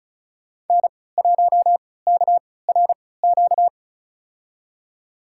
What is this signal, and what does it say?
Morse code, heard as a keyed tone, about 34 wpm, spaced as for 20 wpm, 700 Hz. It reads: N1KRQ
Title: USA Call Signs - 5 Characters - 3x Character Spacing - 35wpm